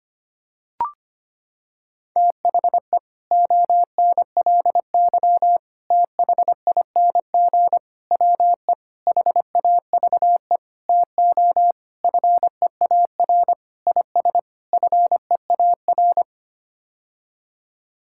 Morse code, heard as a keyed tone, about 25 wpm, 700 Hz. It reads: THE ONLY THING WE HAVE TO FEAR IS FEAR
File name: Sets of 9 Words - 25wpm